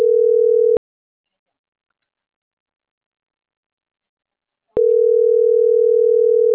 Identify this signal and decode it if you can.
Ringback tone (North America), 440 + 480 Hz.